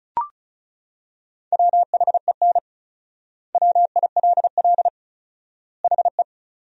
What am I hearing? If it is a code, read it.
Morse code, heard as a keyed tone, about 34 wpm, 700 Hz. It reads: WHEN WILL HE
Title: 4x Word-Spacing - Sentences from Top 200 Words - 35wpm